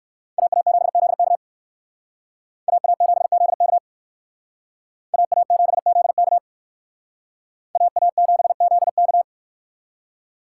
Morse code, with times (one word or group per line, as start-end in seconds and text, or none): 0.4-1.4 AA7ZK
2.7-3.8 AA7ZK
5.1-6.4 AA7ZK
7.8-9.2 AA7ZK